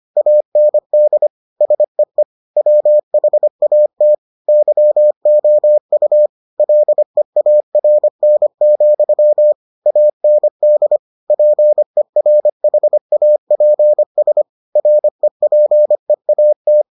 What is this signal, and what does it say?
Morse code, heard as a keyed tone, about 25 wpm, 600 Hz. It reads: AND SEE WHAT YOU LEARN, AND PERHAPS REPEAT